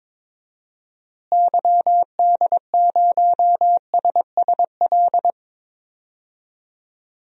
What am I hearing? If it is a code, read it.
Morse code, heard as a keyed tone, about 21 wpm, 700 Hz. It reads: YD0SSL
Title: DXCC-100 Call Signs - 6 Characters - Repeated 2x - 22wpm